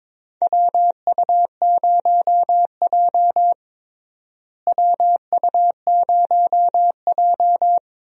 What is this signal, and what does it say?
Morse code, heard as a keyed tone, about 22 wpm, 700 Hz. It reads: WU0J WU0J
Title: USA Call Signs - 4 Characters - Repeated 3x - 22wpm